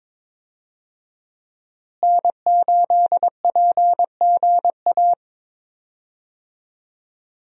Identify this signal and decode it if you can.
Morse code, heard as a keyed tone, about 22 wpm, 700 Hz. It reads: N8PGA
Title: USA Call Signs - 5 Characters - Repeated 2x - 22wpm